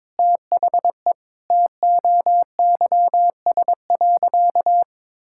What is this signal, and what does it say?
Morse code, heard as a keyed tone, about 22 wpm, 700 Hz. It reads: THE TOYS.
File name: practice / 22wpm / Sentences with 2-Word Phrases - 22wpm